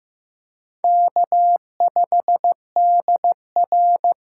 Morse code, heard as a keyed tone, about 15 wpm, 700 Hz. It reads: K5DR
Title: USA Call Signs - 8x Character Spacing - 15wpm